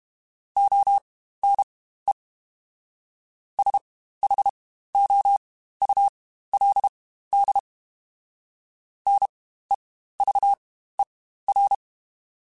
Morse code, heard as a keyed tone, about 32 wpm, spaced as for 15 wpm, 800 Hz. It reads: ONE SHOULD NEVER
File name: Quotes Spaced / Quotes Spaced-mp3s / Advice sp @30